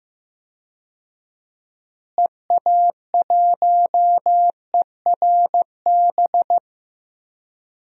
Morse code, heard as a keyed tone, about 15 wpm, 700 Hz. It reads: EA1ERB
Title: DXCC-100 Call Signs - 6 Characters - Repeated 3x - 15wpm